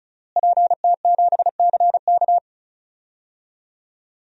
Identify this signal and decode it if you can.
Morse code, heard as a keyed tone, about 36 wpm, 700 Hz. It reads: PT7CK
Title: DXCC-100 Call Signs - 5 Characters - Repeated 2x - 35wpm